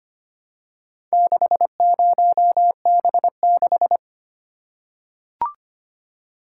Morse code, heard as a keyed tone, about 25 wpm, 700 Hz. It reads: 60B6
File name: practice / 25wpm / Four Letter-Number Combinations - 25wpm